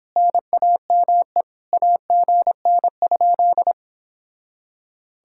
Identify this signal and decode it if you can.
Morse code, heard as a keyed tone, about 29 wpm, 700 Hz. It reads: NAME AGN?